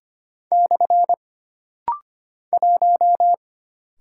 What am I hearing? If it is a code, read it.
Morse code, heard as a keyed tone, about 25 wpm, 700 Hz. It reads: / 1
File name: Single Character (Letter:Number:Punctuation:Prosign) - Rapid-Fire - 25wpm